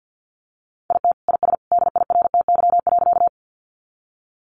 Morse code, heard as a keyed tone, about 49 wpm, 700 Hz. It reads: IT IS DIRTY.
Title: Additional Easy Phrases 50wpm